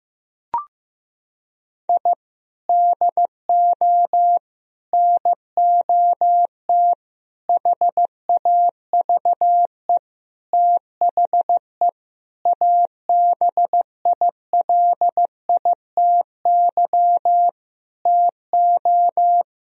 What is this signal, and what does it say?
Morse code, heard as a keyed tone, about 16 wpm, 700 Hz. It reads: I DO NOT HAVE THE ABILITY TO
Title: Sentences from Top 1000 Words - Review Part 3 - 15wpm